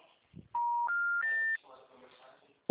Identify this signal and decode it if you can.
Special information tone, with speech.
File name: out- -2051- - - .67223